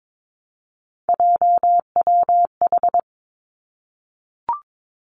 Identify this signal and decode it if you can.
Morse code, heard as a keyed tone, about 22 wpm, 700 Hz. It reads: JWH
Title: Three Challenging Letter-Number Combinations 22wpm